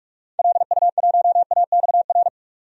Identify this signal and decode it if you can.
Morse code, heard as a keyed tone, about 43 wpm, 700 Hz. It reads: LU1AXR
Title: DXCC-100 Call Signs - 6 Characters - 45wpm